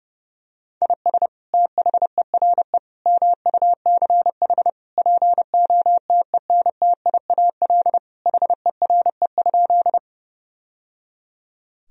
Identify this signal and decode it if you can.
Morse code, heard as a keyed tone, about 32 wpm, 700 Hz. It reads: IS THERE MUCH POTENTIAL HERE?
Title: Sentences from Top 1000 Words - Review Part 4 - 30wpm